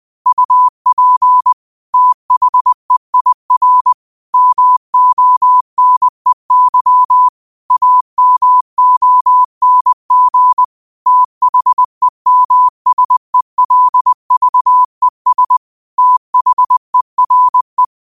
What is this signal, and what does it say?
Morse code, heard as a keyed tone, about 20 wpm, 1000 Hz. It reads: UP THEIR MONEY AMONG THEMSELVES THERE
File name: qotdfiles / 20wpm